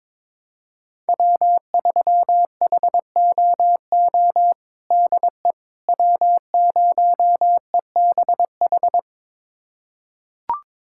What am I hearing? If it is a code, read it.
Morse code, heard as a keyed tone, about 21 wpm, 700 Hz. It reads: W3HOO DE W0EBH
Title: USA Callsign DE Callsign - 5 Characters - 22wpm